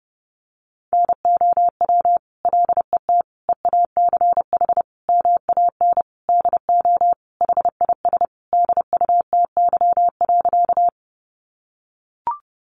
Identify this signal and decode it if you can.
Morse code, heard as a keyed tone, about 30 wpm, 700 Hz. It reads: NOW LET EACH MAN DO HIS DUTY.
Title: Sentences from Top 1500 Words 30wpm